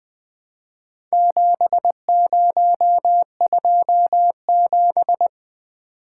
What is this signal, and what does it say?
Morse code, heard as a keyed tone, about 22 wpm, 700 Hz. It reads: 7027
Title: Speed-Racing - Four Letter-Number Combinations - 20wpm